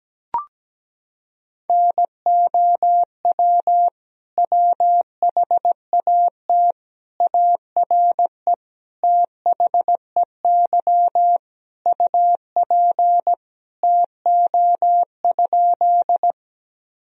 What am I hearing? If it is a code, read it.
Morse code, heard as a keyed tone, about 17 wpm, 700 Hz. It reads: NOW WHAT ARE THEY UP TO?
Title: Sentences from Top 1000 Words - Review Part 1 - 17wpm